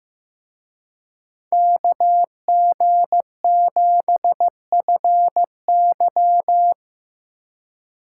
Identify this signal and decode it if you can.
Morse code, heard as a keyed tone, about 18 wpm, 700 Hz. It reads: KG7FY
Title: Speed Racing - USA 2x2 Call Signs 15wpm